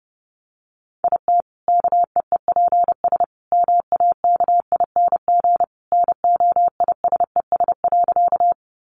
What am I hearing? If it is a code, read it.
Morse code, heard as a keyed tone, about 30 wpm, 700 Hz. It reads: IT KEEPS MAKING NOISES.
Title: Additional Easy Phrases 30wpm